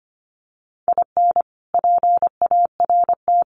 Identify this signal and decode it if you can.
Morse code, heard as a keyed tone, about 25 wpm, 700 Hz. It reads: IN PART
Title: Sets of 5 Words Encore2 25wpm